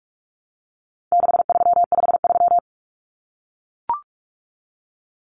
Morse code, heard as a keyed tone, about 44 wpm, 700 Hz. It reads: 6353
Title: Numbers - 4-Digit 45wpm